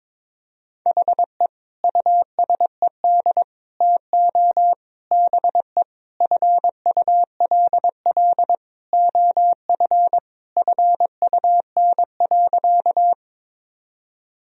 Morse code, heard as a keyed tone, about 23 wpm, 700 Hz. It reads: HE USED TO BE FULL OF FUN.
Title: Sentences from Top 1000 Words - Review Part 4 - 22wpm